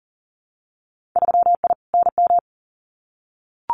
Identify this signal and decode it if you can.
Morse code, heard as a keyed tone, about 41 wpm, 700 Hz. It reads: 3I NM
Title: Winter Field Day Exchanges Encore 40wpm